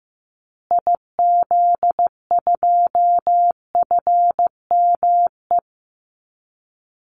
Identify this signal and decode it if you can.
Morse code, heard as a keyed tone, about 15 wpm, 700 Hz. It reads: IZ2FME
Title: Call Signs Repeated 3x - 15wpm